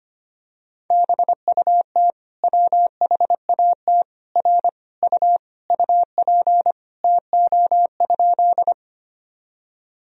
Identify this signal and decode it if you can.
Morse code, heard as a keyed tone, about 25 wpm, 700 Hz. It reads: BUT WHAT R U UP TO?
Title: Sentences from Top 100 Words and Abbreviations 25wpm